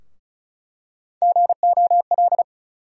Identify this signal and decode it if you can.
Morse code, heard as a keyed tone, about 36 wpm, 700 Hz. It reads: GOL
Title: USA Call Sign Postfixes - 3 Letters 35wpm